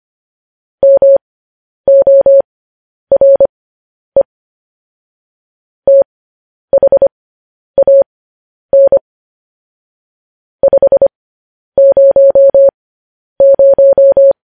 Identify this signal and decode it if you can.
Morse code, heard as a keyed tone, about 25 wpm, spaced as for 10 wpm, 550 Hz. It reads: MORE THAN 500
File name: podcast / News headlines-1x-25x10